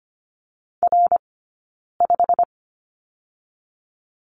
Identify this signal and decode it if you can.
Morse code, heard as a keyed tone, about 25 wpm, 700 Hz. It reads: R 5
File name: Lesson 15d - Number 5 - Two Characters 25wpm